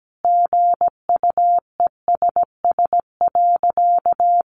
Morse code, heard as a keyed tone, about 17 wpm, 700 Hz. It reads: GUESS.
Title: Sentences from Top 1500 Words 17wpm